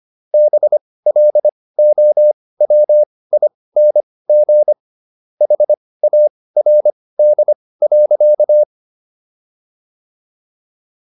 Morse code, heard as a keyed tone, about 25 wpm, spaced as for 18 wpm, 600 Hz. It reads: BLOWING HARD.